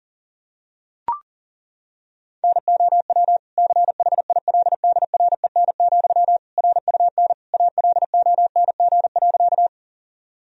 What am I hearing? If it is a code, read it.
Morse code, heard as a keyed tone, about 40 wpm, 700 Hz. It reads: NOW CHILDREN, RUN ALONG.
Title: Sentences from Top 1000 Words - Review Part 1 - 40wpm